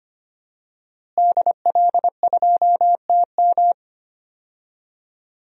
Morse code, heard as a keyed tone, about 25 wpm, 700 Hz. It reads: DL2TM
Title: DXCC-100 Call Signs - 5 Characters - Repeated 3x - 25wpm